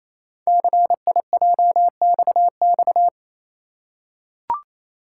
Morse code, heard as a keyed tone, about 27 wpm, 700 Hz. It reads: CIJXX
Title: Code Groups - 5 Letters - 28wpm